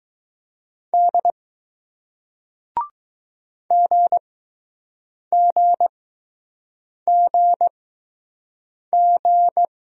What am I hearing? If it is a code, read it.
Morse code, heard as a keyed tone, about 20 wpm, spaced as for 6 wpm, 700 Hz. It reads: D GGGG